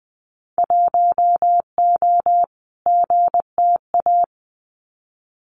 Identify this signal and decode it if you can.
Morse code, heard as a keyed tone, about 20 wpm, 700 Hz. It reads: 1O GTA
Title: Winter Field Day Exchanges 20wpm